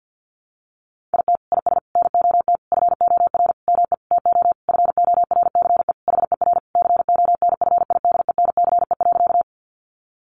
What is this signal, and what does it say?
Morse code, heard as a keyed tone, about 50 wpm, 700 Hz. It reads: IT IS NOT FOR ME TO FORCE HER CONFIDENCE.
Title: Sentences from Top 1500 Words 50wpm